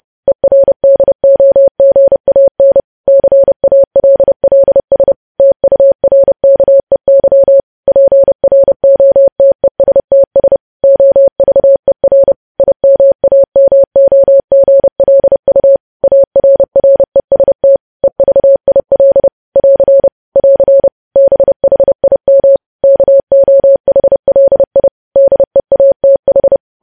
Morse code, heard as a keyed tone, about 30 wpm, 550 Hz. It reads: ERDOGAN CALLS TURKEY PROTESTS OVER IMAMOGLU ARREST EVIL <AR> <AR> BHIM KOHLI DEATH
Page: news headlines in morse code at 30 wpm